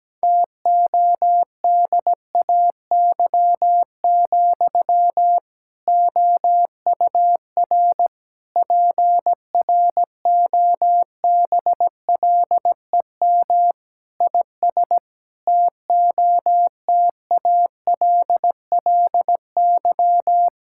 Morse code, heard as a keyed tone, about 17 wpm, 700 Hz. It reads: TODAY, OUR PROBLEM IS TOTALLY
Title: Sentences from Top 1600 Words 17wpm